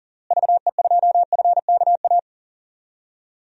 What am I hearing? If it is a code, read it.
Morse code, heard as a keyed tone, about 37 wpm, 700 Hz. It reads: VE2FKA
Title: DXCC-100 Call Signs - 6 Characters - Repeated 3x - 40wpm